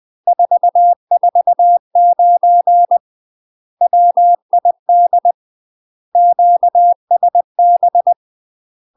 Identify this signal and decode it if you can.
Morse code, heard as a keyed tone, about 20 wpm, 700 Hz. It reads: 449 WID QSB
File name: MASTER QSOs TRAINING/20WPM